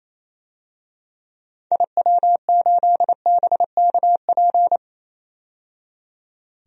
Morse code, heard as a keyed tone, about 28 wpm, 700 Hz. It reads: IW8BKP
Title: DXCC-100 Call Signs - 6 Characters - 28wpm